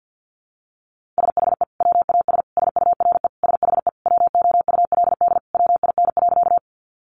Morse code, heard as a keyed tone, about 50 wpm, 700 Hz. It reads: SHE WAS SURE SHE WOULD WIN.
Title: Sentences from Top 1500 Words 50wpm